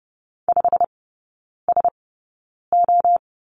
Morse code, heard as a keyed tone, about 30 wpm, spaced as for 9 wpm, 690 Hz. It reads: 5SO
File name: Lesson 17e - Letter u - Three Characters 30wpm